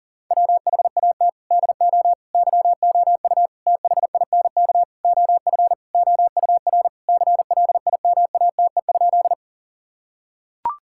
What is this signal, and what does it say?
Morse code, heard as a keyed tone, about 40 wpm, 700 Hz. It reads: WHAT DO YOU THINK OF OUR CLIMATE?
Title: Sentences from Top 1600 Words 40wpm